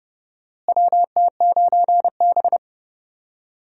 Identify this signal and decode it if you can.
Morse code, heard as a keyed tone, about 30 wpm, 700 Hz. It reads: WT9B
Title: USA Call Signs - 4 Characters - Repeated 3x - 30wpm